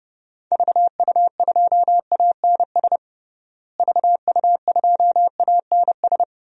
Morse code, heard as a keyed tone, about 30 wpm, 700 Hz. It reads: VU2ANS VU2ANS
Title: DXCC-100 Call Signs - 6 Characters - Repeated 3x - 30wpm